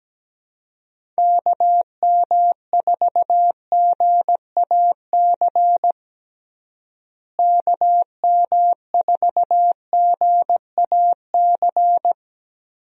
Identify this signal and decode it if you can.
Morse code, heard as a keyed tone, about 17 wpm, 700 Hz. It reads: KM4GAC KM4GAC
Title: USA Call Signs - 6 Characters - Repeated 3x - 17wpm